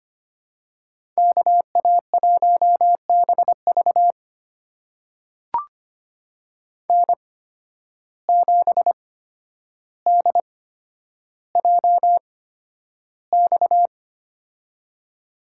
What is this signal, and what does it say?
Morse code, heard as a keyed tone, about 25 wpm, spaced as for 11 wpm, 700 Hz. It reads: KA1BV N7DJX
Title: USA Call Signs - 5 Characters - 8x Character Spacing - 25wpm